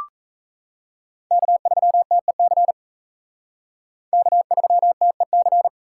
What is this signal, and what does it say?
Morse code, heard as a keyed tone, about 35 wpm, 700 Hz. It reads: K3TEC K3TEC